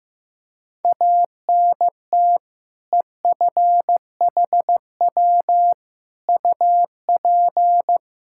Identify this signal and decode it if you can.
Morse code, heard as a keyed tone, about 15 wpm, 700 Hz. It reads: ANT EFHW UP
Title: Practice QSO - Exchange 2 - Info - 15wpm